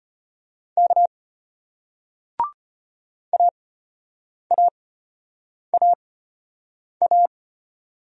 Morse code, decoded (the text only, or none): K AAAA